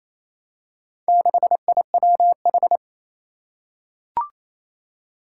Morse code, heard as a keyed tone, about 28 wpm, 700 Hz. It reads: 6IWH
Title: Four Challenging Letter-Number Combinations 28wpm